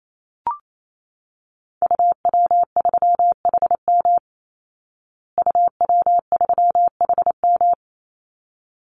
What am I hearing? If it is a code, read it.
Morse code, heard as a keyed tone, about 28 wpm, 690 Hz. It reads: UW3HM UW3HM
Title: Call Signs Repeated 3x - 28wpm